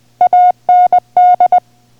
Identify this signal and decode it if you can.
Morse code, heard as a keyed tone, about 20 wpm, 720 Hz. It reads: AND